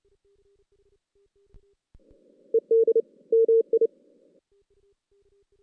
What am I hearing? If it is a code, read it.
Morse code, heard as a keyed tone, about 28 wpm, 450 Hz. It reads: ED MI
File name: 2016 JAN CW NAQP / 2016 Jan naqp k5ot - 0-5